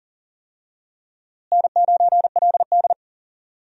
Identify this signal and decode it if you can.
Morse code, heard as a keyed tone, about 40 wpm, 700 Hz. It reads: N9LD
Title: USA Call Signs - 4 Characters - Repeated 3x - 40wpm